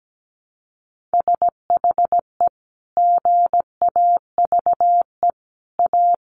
Morse code, heard as a keyed tone, about 17 wpm, 700 Hz. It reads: SHE GAVE A
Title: Sentences from Top 1300 Words 17wpm